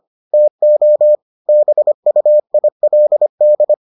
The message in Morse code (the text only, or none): TO BUILD